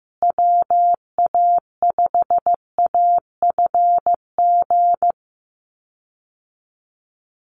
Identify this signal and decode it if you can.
Morse code, heard as a keyed tone, about 15 wpm, 700 Hz. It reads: WA5AFG
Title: USA Call Signs Repeated 2x 15wpm